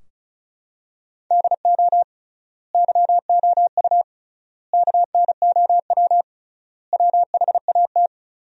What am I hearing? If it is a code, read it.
Morse code, heard as a keyed tone, about 34 wpm, 700 Hz. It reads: DO YOU KNOW WHAT
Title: 3x Word-Spacing - Sentences from Top 100 Words - 35wpm